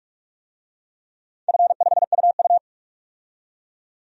Morse code, heard as a keyed tone, about 46 wpm, 700 Hz. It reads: F5UU
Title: DXCC-100 Call Signs - 4 Characters - 45wpm